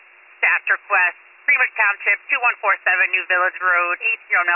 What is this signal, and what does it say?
Emergency Alert System (SAME) tones, heard as harsh buzzing bursts, with speech. It is heard from 0.3 to 3.7 s.